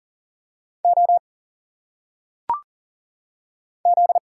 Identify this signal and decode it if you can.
Morse code, heard as a keyed tone, about 40 wpm, 700 Hz. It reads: O Z
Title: Single Letters - Phonetic Pronunciation 40wpm